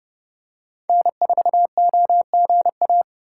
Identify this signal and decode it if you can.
Morse code, heard as a keyed tone, about 30 wpm, 700 Hz. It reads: N4OGA
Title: USA Call Signs - 5 Characters - 3x Character Spacing - 30wpm